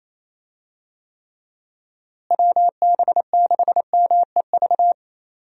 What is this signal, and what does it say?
Morse code, heard as a keyed tone, about 28 wpm, 700 Hz. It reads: WB6MEV
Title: USA Call Signs - 6 Characters - Repeated 3x - 28wpm